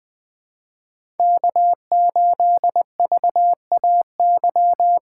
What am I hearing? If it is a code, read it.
Morse code, heard as a keyed tone, about 20 wpm, 700 Hz. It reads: K8VAY
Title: USA Call Signs - 5 Characters - 3x Character Spacing - 20wpm